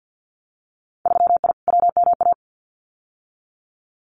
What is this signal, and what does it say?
Morse code, heard as a keyed tone, about 50 wpm, 700 Hz. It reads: 3I WMA
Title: Winter Field Day Exchanges Encore 50wpm